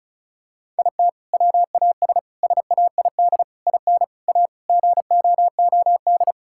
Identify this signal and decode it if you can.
Morse code, heard as a keyed tone, about 36 wpm, 700 Hz. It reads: IT WAS SAID IN A GOOD
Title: Sentences from Top 1600 Words 35wpm